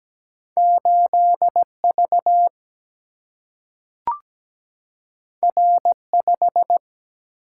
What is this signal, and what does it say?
Morse code, heard as a keyed tone, about 17 wpm, 700 Hz. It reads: 8 V R 5
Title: Single-Letter Single-Number 17wpm